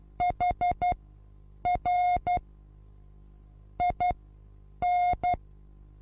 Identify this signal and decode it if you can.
Morse code, heard as a keyed tone, about 12 wpm, spaced as for 8 wpm, 720 Hz. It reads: HR IN
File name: Onlinelehrgang / morsen / DJ4UF Morsen23